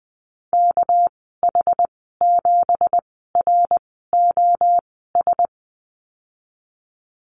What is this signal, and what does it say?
Morse code, heard as a keyed tone, about 20 wpm, spaced as for 15 wpm, 700 Hz. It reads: KH7ROS
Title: 207 Lesson 33g - Letter j - Call Signs 20wpm